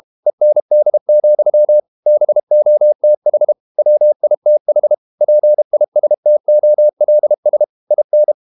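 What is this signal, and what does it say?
Morse code, heard as a keyed tone, about 32 wpm, 600 Hz. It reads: END, BOTH WITH PISTOLS IN
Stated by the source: Morse code practice at 32 wpm from CW med Gnister, reading Ken Follett's novel Never in English.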